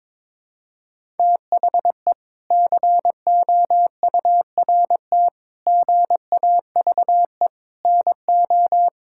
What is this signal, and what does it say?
Morse code, heard as a keyed tone, about 21 wpm, 700 Hz. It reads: THE COURT GAVE NO
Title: Sentences from Top 1600 Words 22wpm